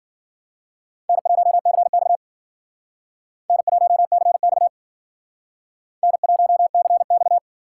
Morse code, heard as a keyed tone, about 53 wpm, 700 Hz. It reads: N1CX N1CX N1CX